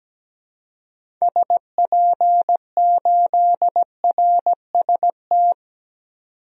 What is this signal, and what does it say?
Morse code, heard as a keyed tone, about 17 wpm, 700 Hz. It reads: SP8RST
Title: DXCC-100 Call Signs - 6 Characters - 17wpm